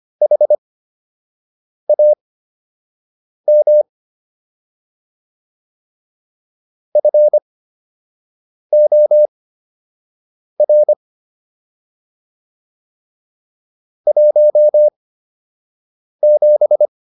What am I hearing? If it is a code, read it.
Morse code, heard as a keyed tone, about 25 wpm, spaced as for 6 wpm, 600 Hz. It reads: HAM FOR 17